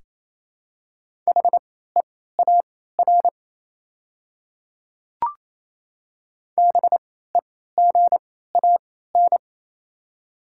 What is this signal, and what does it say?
Morse code, heard as a keyed tone, about 28 wpm, spaced as for 16 wpm, 700 Hz. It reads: HEAR BEGAN
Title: 3x Character Spacing - Top 300 Words - 28wpm